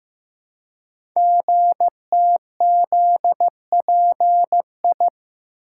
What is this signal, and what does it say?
Morse code, heard as a keyed tone, about 15 wpm, 700 Hz. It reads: GTZPI
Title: Code Groups - 5 Letters - 15wpm